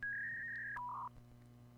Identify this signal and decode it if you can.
Fax machine handshake.